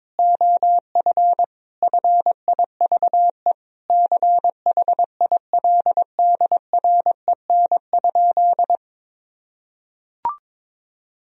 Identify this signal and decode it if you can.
Morse code, heard as a keyed tone, about 22 wpm, 700 Hz. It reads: OF FIVE CHILDREN?
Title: Sentences from Top 1000 Words - Review Part 2 - 22wpm